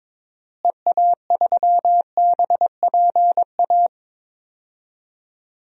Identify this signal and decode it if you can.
Morse code, heard as a keyed tone, about 22 wpm, 700 Hz. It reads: EA3BPA
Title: DXCC-100 Call Signs - 6 Characters - Repeated 2x - 22wpm